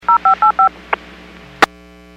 DTMF, heard as telephone touch tones, tones of about 110 ms. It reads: 0585
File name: WPS / 05964 17-07 WPS 451.25 WPS